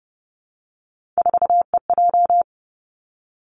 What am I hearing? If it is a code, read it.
Morse code, heard as a keyed tone, about 30 wpm, 700 Hz. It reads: 4EJ
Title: Three Challenging Letter-Number Combinations 30wpm